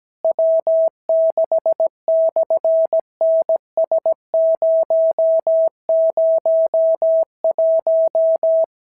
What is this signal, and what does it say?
Morse code, heard as a keyed tone, about 17 wpm, 650 Hz. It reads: W6/NS001
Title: SOTA QSO Practice 17wpm